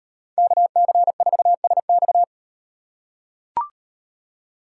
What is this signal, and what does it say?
Morse code, heard as a keyed tone, about 38 wpm, 700 Hz. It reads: KC4SX